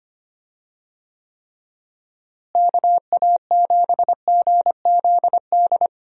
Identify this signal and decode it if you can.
Morse code, heard as a keyed tone, about 25 wpm, 700 Hz. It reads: KA7GZD
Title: USA Call Signs - 6 Characters - Repeated 2x - 25wpm